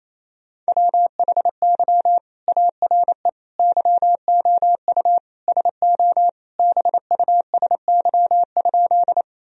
Morse code, heard as a keyed tone, about 28 wpm, 700 Hz. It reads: WHY ARE YOU SO BUSY?
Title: Sentences from Top 1000 Words - Review Part 3 - 28wpm